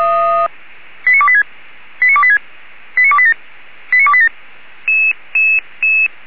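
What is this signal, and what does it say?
Five-tone selective call (ZVEI), heard as quick runs of five tones, tones of about 70 ms. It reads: 89286 89286 89287 89287
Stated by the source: Five-tone ZVEI selective call 89299, received on SDR.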